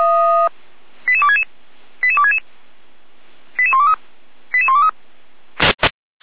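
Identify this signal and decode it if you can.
Five-tone selective call (ZVEI), heard as quick runs of five tones, tones of about 70 ms. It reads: 88266 88266 88123 88123
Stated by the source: ZVEI five-tone selective call received by SDR.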